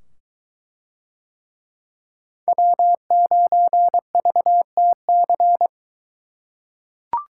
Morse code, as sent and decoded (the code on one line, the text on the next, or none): .-- ----. ...- - -.-.
W9VTC